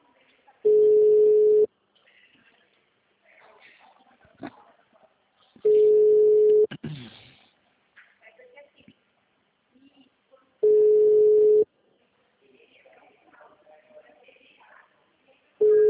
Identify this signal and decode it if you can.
Ringback tone (Europe, 425 Hz), with speech.
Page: Itu 2